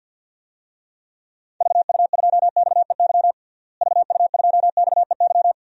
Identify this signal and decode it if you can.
Morse code, heard as a keyed tone, about 47 wpm, 700 Hz. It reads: VU2XEY VU2XEY
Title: DXCC-100 Call Signs - 6 Characters - Repeated 2x - 50wpm